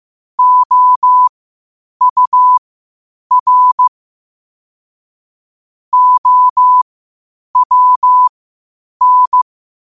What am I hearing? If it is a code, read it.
Morse code, heard as a keyed tone, about 15 wpm, spaced as for 8 wpm, 1000 Hz. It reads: OUR OWN
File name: qotdfiles / 8wpm